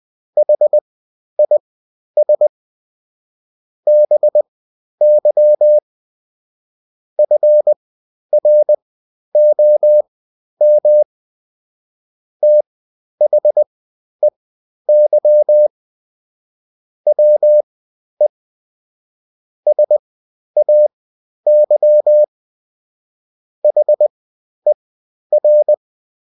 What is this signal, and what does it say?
Morse code, heard as a keyed tone, about 20 wpm, spaced as for 11 wpm, 600 Hz. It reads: HIS BY FROM THEY WE SAY HER
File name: Top-100-Words-10